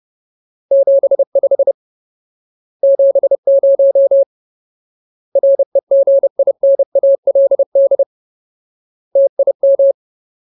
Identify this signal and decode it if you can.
Morse code, heard as a keyed tone, about 30 wpm, 550 Hz. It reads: 75 70 REGINALD TIM
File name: Ultimate Word List / 30WPM Ultimate